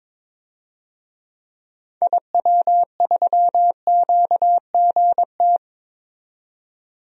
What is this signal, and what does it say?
Morse code, heard as a keyed tone, about 22 wpm, 700 Hz. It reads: IW3QGT
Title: DXCC-100 Call Signs - 6 Characters - 22wpm